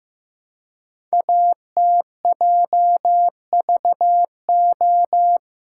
Morse code, heard as a keyed tone, about 15 wpm, 700 Hz. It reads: ATJVO
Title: Code Groups - 5 Letters - 15wpm